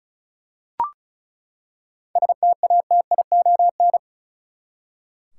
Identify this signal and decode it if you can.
Morse code, heard as a keyed tone, about 34 wpm, 700 Hz. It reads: STATION
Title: Seven Letter Words - 35wpm